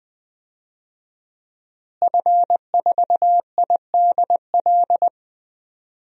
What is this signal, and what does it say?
Morse code, heard as a keyed tone, about 20 wpm, 700 Hz. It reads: F4IDL
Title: DXCC-100 Call Signs - 5 Characters - Repeated 2x - 20wpm